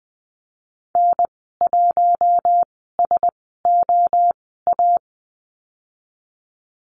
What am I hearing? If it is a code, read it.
Morse code, heard as a keyed tone, about 20 wpm, spaced as for 15 wpm, 700 Hz. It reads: N1SOA
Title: 069 Lesson 12g - Letter d - Call Signs 20wpm